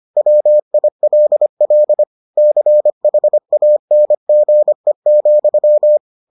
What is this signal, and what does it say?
Morse code, heard as a keyed tone, about 25 wpm, 600 Hz. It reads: WILL CHANGE,